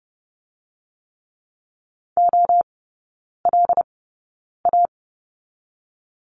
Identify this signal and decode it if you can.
Morse code, heard as a keyed tone, about 30 wpm, spaced as for 9 wpm, 700 Hz. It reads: OLA
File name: Lesson 18e - Letter m - Three Characters 30wpm